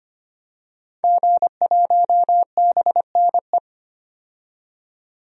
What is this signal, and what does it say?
Morse code, heard as a keyed tone, about 25 wpm, 700 Hz. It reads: G1BNE
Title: DXCC-100 Call Signs - 5 Characters - 25wpm